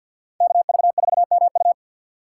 Morse code, heard as a keyed tone, about 62 wpm, 690 Hz. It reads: KV4MU
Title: Speed Racing - USA 2x2 Call Signs 50wpm